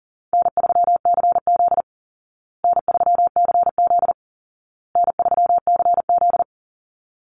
Morse code, heard as a keyed tone, about 40 wpm, 700 Hz. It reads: N3CZ N3CZ N3CZ
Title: Easy Call Signs Repeated 3x - 40wpm